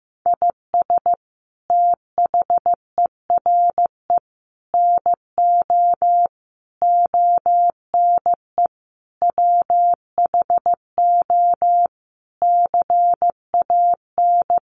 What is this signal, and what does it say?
Morse code, heard as a keyed tone, about 15 wpm, 700 Hz. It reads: IS THERE NO ONE WHO CAN
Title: Sentences from Top 200 Words 15wpm